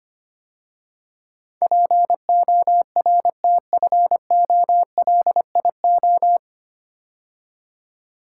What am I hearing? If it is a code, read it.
Morse code, heard as a keyed tone, about 25 wpm, 700 Hz. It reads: PORTFOLIO
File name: practice / 25wpm / 9 Letter Words - First 3 Letters Given - 25wpm